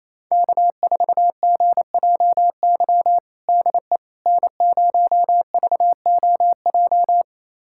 Morse code, heard as a keyed tone, about 28 wpm, 700 Hz. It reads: K4GJY DE N0VOJ
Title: USA Callsign DE Callsign - 5 Characters - 28wpm